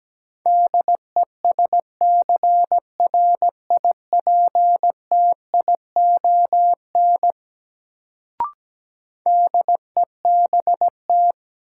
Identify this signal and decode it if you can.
Morse code, heard as a keyed tone, about 17 wpm, 700 Hz. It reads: DESCRIPTION DEBT
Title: Sentences from Top 1300 Words - New Word Drill - 17wpm